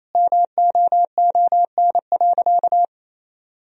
Morse code, heard as a keyed tone, about 28 wpm, 700 Hz. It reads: MOON.
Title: Sentences from Top 1000 Words - Review Part 3 - 28wpm